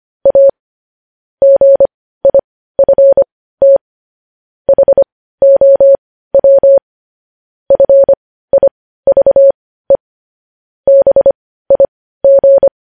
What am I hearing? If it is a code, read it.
Morse code, heard as a keyed tone, about 25 wpm, spaced as for 15 wpm, 550 Hz. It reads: A GIFT HOW FIVE BIG
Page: news headlines in morse code at 15 wpm